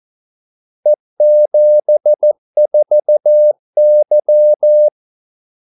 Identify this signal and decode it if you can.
Morse code, heard as a keyed tone, about 14 wpm, 600 Hz. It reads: E74Y
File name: Callsigns-14wpm